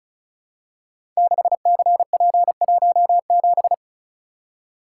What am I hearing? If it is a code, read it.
Morse code, heard as a keyed tone, about 36 wpm, 690 Hz. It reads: 6CP17
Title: Code Groups - 5 Letter:Numbers - 35wpm